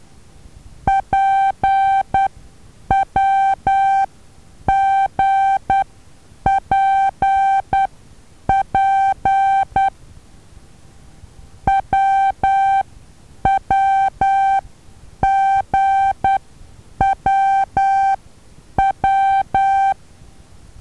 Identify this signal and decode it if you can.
Morse code, heard as a keyed tone, about 10 wpm, spaced as for 8 wpm, 780 Hz. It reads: PWGPP WWGWW
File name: Taviro HA6VA B